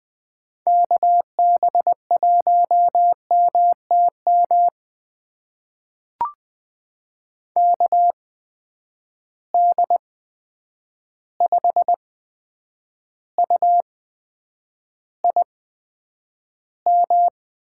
Morse code, heard as a keyed tone, about 20 wpm, spaced as for 9 wpm, 700 Hz. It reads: KB1MTM KD5UIM
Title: USA Call Signs - 6 Characters - 8x Character Spacing - 20wpm